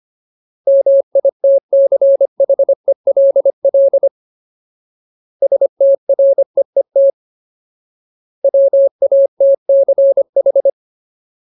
Morse code, heard as a keyed tone, about 25 wpm, 550 Hz. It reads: MITCHELL STREET WATCH
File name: Ultimate Word List / 25WPM Ultimate